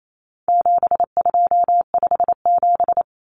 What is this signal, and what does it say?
Morse code, heard as a keyed tone, about 28 wpm, 700 Hz. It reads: 7257
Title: Numbers - 4-Digit 28wpm